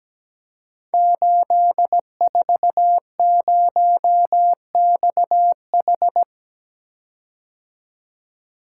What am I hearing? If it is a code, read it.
Morse code, heard as a keyed tone, about 17 wpm, 700 Hz. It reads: 840XH
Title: Code Groups - 5 Letter:Numbers - 17wpm